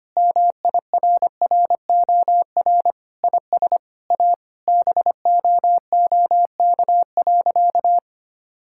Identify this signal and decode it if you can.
Morse code, heard as a keyed tone, about 25 wpm, 700 Hz. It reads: MIRROR IS A BOOK.
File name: practice / 25wpm / Sentences with 4-Word Phrases - 25wpm